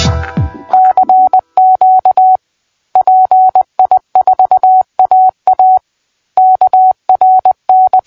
Morse code, heard as a keyed tone, about 20 wpm, 750 Hz. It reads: CQ PI4AA KRN
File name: PI4AA Uitzending